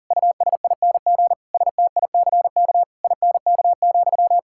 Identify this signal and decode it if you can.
Morse code, heard as a keyed tone, about 40 wpm, 700 Hz. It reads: USING STICK INK,
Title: Pangram Challenge v2 - 40wpm